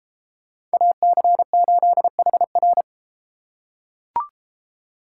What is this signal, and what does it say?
Morse code, heard as a keyed tone, about 32 wpm, 700 Hz. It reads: AC8HR